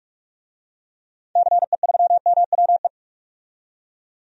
Morse code, heard as a keyed tone, about 46 wpm, 700 Hz. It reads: CE3MWE
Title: DXCC-100 Call Signs - 6 Characters - 45wpm